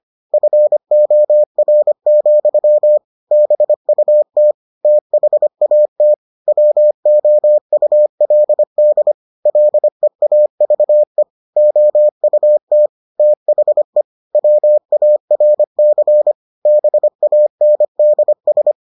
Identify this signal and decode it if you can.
Morse code, heard as a keyed tone, about 25 wpm, 600 Hz. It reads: FOR, BUT THAT WOULD LEAVE OUT THE WARC BANDS